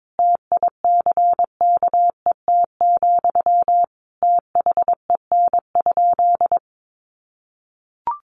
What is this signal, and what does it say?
Morse code, heard as a keyed tone, about 22 wpm, 700 Hz. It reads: TICKET, THEN?
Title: Sentences from Top 1400 Words 22wpm